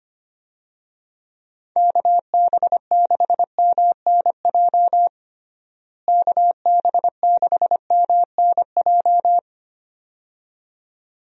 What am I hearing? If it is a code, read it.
Morse code, heard as a keyed tone, about 25 wpm, 700 Hz. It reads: KB6MNJ KB6MNJ
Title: USA Call Signs - 6 Characters - Repeated 2x - 25wpm